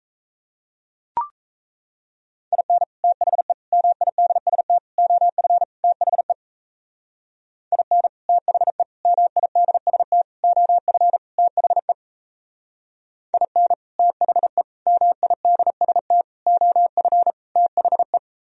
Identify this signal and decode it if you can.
Morse code, heard as a keyed tone, about 34 wpm, 700 Hz. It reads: IN THE MIDST OF THE IN THE MIDST OF THE IN THE MIDST OF THE
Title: Speed-Racing - Sets of 5 Words 28wpm